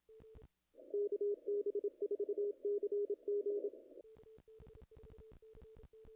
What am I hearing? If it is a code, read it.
Morse code, heard as a keyed tone, about 27 wpm, 410 Hz. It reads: KB4CG
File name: contests / ss-cw-k5tr-op-n5rz -0000